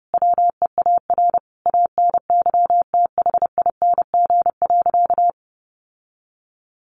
Morse code, heard as a keyed tone, about 30 wpm, 700 Hz. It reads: WEAR ANYTHING.
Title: Sentences from Top 1500 Words 30wpm